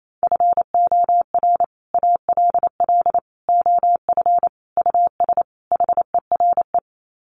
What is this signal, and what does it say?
Morse code, heard as a keyed tone, about 28 wpm, 700 Hz. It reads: FOR ALL OF US HERE
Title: Sets of 5 Words 28wpm